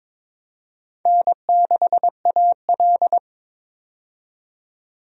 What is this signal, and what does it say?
Morse code, heard as a keyed tone, about 22 wpm, 700 Hz. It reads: N6AL
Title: USA Call Signs - 4 Characters - 22wpm